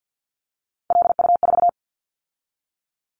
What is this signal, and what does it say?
Morse code, heard as a keyed tone, about 50 wpm, 700 Hz. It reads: LU4
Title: Callsign Prefixes for Top 50 DXCC - 50wpm